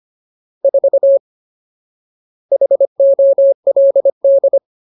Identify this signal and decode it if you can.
Morse code, heard as a keyed tone, about 25 wpm, 550 Hz. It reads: 4 HOLD
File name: Ultimate Word List / 25WPM Ultimate